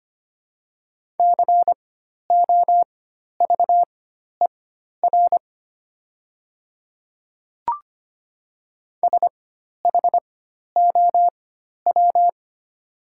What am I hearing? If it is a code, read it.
Morse code, heard as a keyed tone, about 25 wpm, spaced as for 12 wpm, 700 Hz. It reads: COVER SHOW
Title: 4x Character Spacing - Top 500 Words - 25wpm